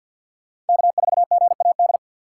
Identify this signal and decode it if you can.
Morse code, heard as a keyed tone, about 49 wpm, 700 Hz. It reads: K4GAD
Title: USA Call Signs - 5 Characters - 2x Character Spacing - 50wpm